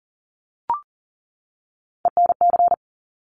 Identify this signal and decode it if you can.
Morse code, heard as a keyed tone, about 39 wpm, 700 Hz. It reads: ENC
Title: Top Three Letter Combinations in Words 40wpm